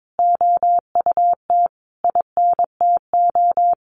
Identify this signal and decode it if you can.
Morse code, heard as a keyed tone, about 22 wpm, 700 Hz. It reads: OUT INTO
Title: Sentences from Top 700 Words 22wpm